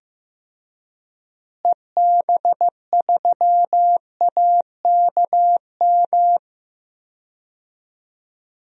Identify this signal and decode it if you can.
Morse code, heard as a keyed tone, about 15 wpm, 700 Hz. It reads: EB3AKM
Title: DXCC-100 Call Signs - 6 Characters - Repeated 3x - 15wpm